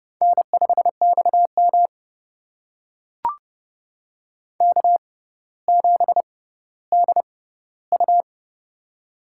Morse code, heard as a keyed tone, about 30 wpm, spaced as for 15 wpm, 700 Hz. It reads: N5XM K7DU
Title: USA Call Signs - 6x Character Spacing - 30wpm